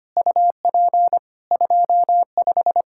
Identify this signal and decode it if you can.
Morse code, heard as a keyed tone, about 25 wpm, 700 Hz. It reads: UP 25
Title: Practice QSO - Exchange 2 - Info - 25wpm